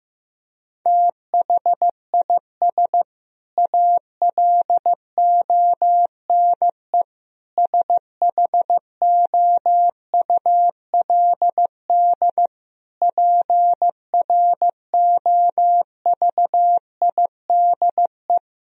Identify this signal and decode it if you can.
Morse code, heard as a keyed tone, about 15 wpm, 700 Hz. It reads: THIS ALONE SHOULD PROVIDE
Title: Sentences from Top 1000 Words - Review Part 4 - 15wpm